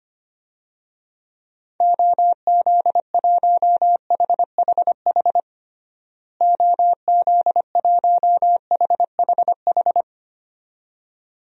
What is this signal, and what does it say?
Morse code, heard as a keyed tone, about 25 wpm, 700 Hz. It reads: OZ1HHH OZ1HHH
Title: DXCC-100 Call Signs - 6 Characters - Repeated 2x - 25wpm